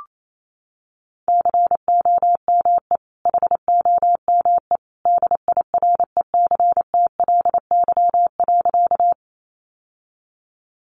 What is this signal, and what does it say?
Morse code, heard as a keyed tone, about 28 wpm, 700 Hz. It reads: COME HOME DIRECTLY.
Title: Sentences from Top 800 Words 28wpm